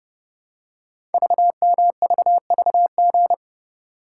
Morse code, heard as a keyed tone, about 29 wpm, 700 Hz. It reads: VMVVG
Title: Code Groups - 5 Letters - 30wpm